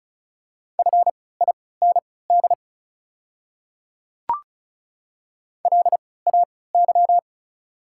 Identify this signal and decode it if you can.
Morse code, heard as a keyed tone, about 34 wpm, spaced as for 20 wpm, 690 Hz. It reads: FIND LAY
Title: 3x Character Spacing - Top 500 Words - 35wpm